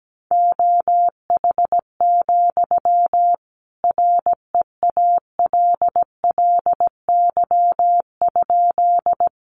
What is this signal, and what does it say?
Morse code, heard as a keyed tone, about 17 wpm, 700 Hz. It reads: OH, REALLY?
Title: Sentences from Top 600 Words 17wpm